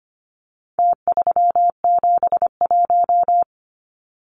Morse code, heard as a keyed tone, about 25 wpm, 700 Hz. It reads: T371
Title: Numbers - Contest Style - 4-Digit 25wpm